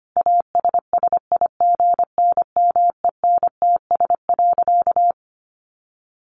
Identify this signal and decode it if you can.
Morse code, heard as a keyed tone, about 25 wpm, 700 Hz. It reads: ASSIGNMENTS.
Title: Easy Phrases 25wpm